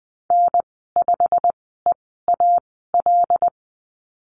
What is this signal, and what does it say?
Morse code, heard as a keyed tone, about 20 wpm, spaced as for 15 wpm, 700 Hz. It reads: N5EAL
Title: 088 Lesson 15g - Number 5 - Call Signs 20wpm